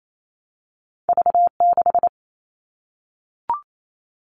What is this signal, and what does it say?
Morse code, heard as a keyed tone, about 28 wpm, 700 Hz. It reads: V6
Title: All Cut Numbers - 2-Digit 28wpm